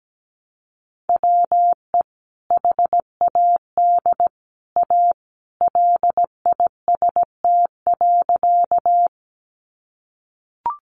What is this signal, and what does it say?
Morse code, heard as a keyed tone, about 17 wpm, 700 Hz. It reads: WE HAD A LIST.
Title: Sentences from Top 400 Words 17wpm